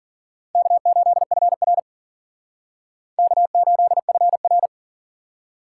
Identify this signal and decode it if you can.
Morse code, heard as a keyed tone, about 51 wpm, 700 Hz. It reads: K8FR K8FR